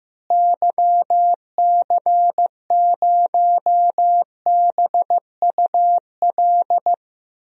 Morse code, heard as a keyed tone, about 15 wpm, 700 Hz. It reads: YC0BUL
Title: DXCC-100 Call Signs - 6 Characters - Repeated 3x - 15wpm